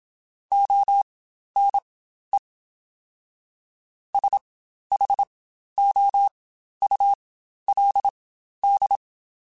Morse code, heard as a keyed tone, about 27 wpm, spaced as for 12 wpm, 800 Hz. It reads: ONE SHOULD
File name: Quotes Spaced / Quotes Spaced-mp3s / Advice sp @25